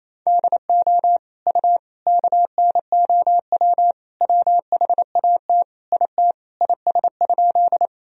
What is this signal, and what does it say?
Morse code, heard as a keyed tone, about 28 wpm, 700 Hz. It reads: DO U KNOW WHAT IT IS?
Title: Sentences from Top 100 Words and Abbreviations 28wpm